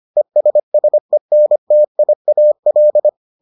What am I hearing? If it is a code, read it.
Morse code, heard as a keyed tone, about 25 wpm, 600 Hz. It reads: ESSENTIAL